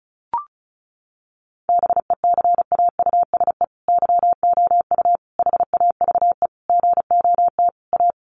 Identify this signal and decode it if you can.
Morse code, heard as a keyed tone, about 34 wpm, 700 Hz. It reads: BECAUSE YOU HAVE GOT A
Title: Sentences from Top 1500 Words 35wpm